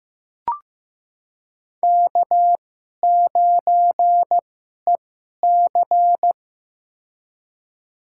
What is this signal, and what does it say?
Morse code, heard as a keyed tone, about 15 wpm, spaced as for 11 wpm, 700 Hz. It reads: K9EC
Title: USA Call Signs - 2x Character Spacing - 15wpm